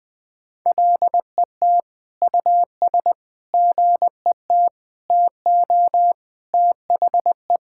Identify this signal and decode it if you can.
Morse code, heard as a keyed tone, about 20 wpm, 700 Hz. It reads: LET US GET TO THE
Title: Sentences from Top 1600 Words 20wpm